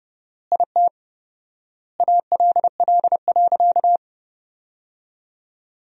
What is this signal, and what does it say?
Morse code, heard as a keyed tone, about 30 wpm, 700 Hz. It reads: IT ALL.
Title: 4x Word-Spacing - Sentences from Top 200 Words - 30wpm